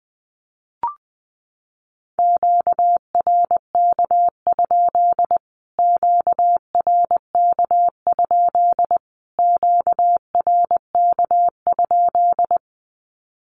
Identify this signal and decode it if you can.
Morse code, heard as a keyed tone, about 20 wpm, 700 Hz. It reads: QRK? QRK? QRK?
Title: Q-Codes Repeated 3x 20wpm